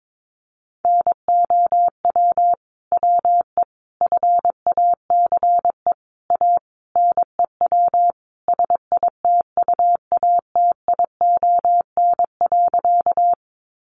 Morse code, heard as a keyed tone, about 22 wpm, 700 Hz. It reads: NOW WE FACE A NEW SITUATION.
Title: Sentences from Top 1200 Words 22wpm